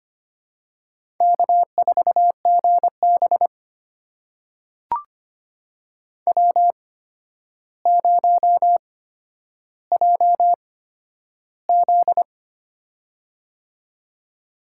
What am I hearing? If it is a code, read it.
Morse code, heard as a keyed tone, about 25 wpm, spaced as for 11 wpm, 700 Hz. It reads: K4GB W0JZ
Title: USA Call Signs - 4 Characters - 8x Character Spacing - 25wpm